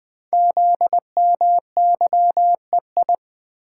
Morse code, heard as a keyed tone, about 20 wpm, 700 Hz. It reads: ZMYEI
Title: Code Groups - 5 Letters - 20wpm